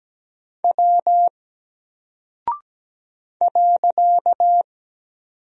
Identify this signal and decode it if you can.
Morse code, heard as a keyed tone, about 17 wpm, 700 Hz. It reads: W .
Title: Single Character (Letter:Number:Punctuation:Prosign) Alt 17wpm